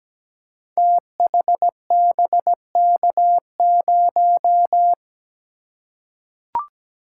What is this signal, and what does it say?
Morse code, heard as a keyed tone, about 17 wpm, 700 Hz. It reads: THBK0
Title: Code Groups - 5 Letter:Numbers - 17wpm